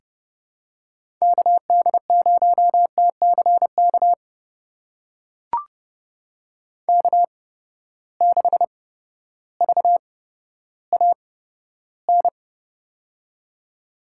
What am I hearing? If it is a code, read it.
Morse code, heard as a keyed tone, about 30 wpm, spaced as for 14 wpm, 700 Hz. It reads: KD0TCK K6VAN
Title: USA Call Signs - 8x Character Spacing - 30wpm